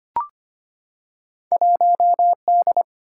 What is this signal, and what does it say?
Morse code, heard as a keyed tone, about 25 wpm, 700 Hz. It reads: 1D
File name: Two Letter-Number Combinations - 25wpm